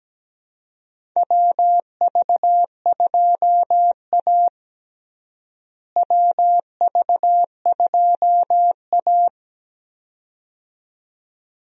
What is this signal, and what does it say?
Morse code, heard as a keyed tone, about 17 wpm, 700 Hz. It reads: WV2A WV2A
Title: USA Call Signs - 4 Characters - Repeated 2x - 17wpm